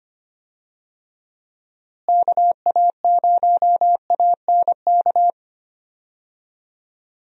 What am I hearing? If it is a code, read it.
Morse code, heard as a keyed tone, about 25 wpm, 700 Hz. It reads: KA0ANK
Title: USA Call Signs - 6 Characters - Repeated 3x - 25wpm